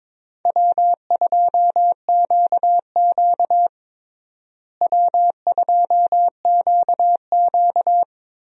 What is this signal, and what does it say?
Morse code, heard as a keyed tone, about 22 wpm, 700 Hz. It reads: W2QQ W2QQ
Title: USA Call Signs - 4 Characters - Repeated 3x - 22wpm